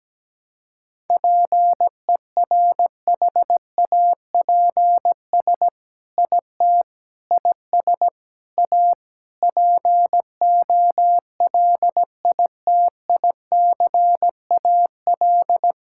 Morse code, heard as a keyed tone, about 17 wpm, 700 Hz. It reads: PERHAPS IT IS A POLITICAL
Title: Sentences from Top 1000 Words - Review Part 5 - 17wpm